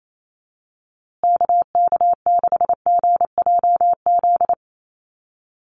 Morse code, heard as a keyed tone, about 28 wpm, 700 Hz. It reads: KK6GJZ
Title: USA Call Signs Repeated 3x 28wpm